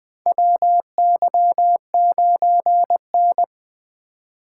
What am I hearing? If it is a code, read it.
Morse code, heard as a keyed tone, about 20 wpm, 700 Hz. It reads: WY9N
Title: USA Call Signs - 4 Characters - 2x Character Spacing - 20wpm